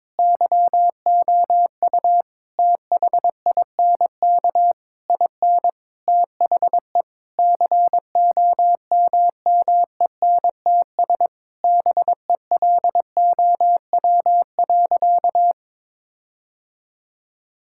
Morse code, heard as a keyed tone, about 22 wpm, 700 Hz. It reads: YOU THINK IN THE COMMENTS BELOW.
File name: Sentences with 9-Word Phrases - 22wpm